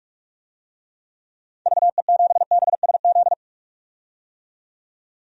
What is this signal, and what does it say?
Morse code, heard as a keyed tone, about 45 wpm, 700 Hz. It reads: VE7BSZ
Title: DXCC-100 Call Signs - 6 Characters - 45wpm